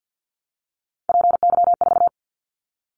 Morse code, heard as a keyed tone, about 50 wpm, 700 Hz. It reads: PY4
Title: Callsign Prefixes for Top 100 DXCC - 50wpm